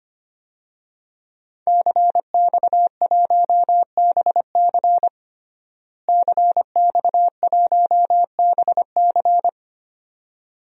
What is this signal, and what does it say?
Morse code, heard as a keyed tone, about 25 wpm, 700 Hz. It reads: CX1BC CX1BC
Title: DXCC-100 Call Signs - 5 Characters - Repeated 2x - 25wpm